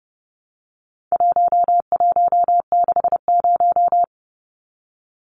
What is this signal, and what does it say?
Morse code, heard as a keyed tone, about 30 wpm, 700 Hz. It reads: 1160
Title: Numbers - 4-Digit 30wpm